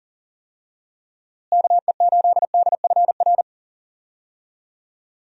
Morse code, heard as a keyed tone, about 40 wpm, 700 Hz. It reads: KE8DFR
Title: USA Call Signs - 6 Characters - Repeated 3x - 40wpm